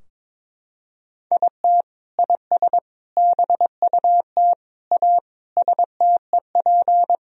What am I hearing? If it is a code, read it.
Morse code, heard as a keyed tone, about 22 wpm, 700 Hz. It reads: IT IS BUT A STEP
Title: Sentences from Top 1000 Words - Review Part 2 - 22wpm